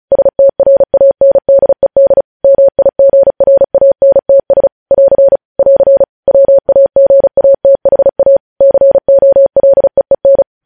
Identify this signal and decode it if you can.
Morse code, heard as a keyed tone, about 36 wpm, 550 Hz. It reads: STRANDED MIGRANTS <AR> <AR> WAGATHA COLEEN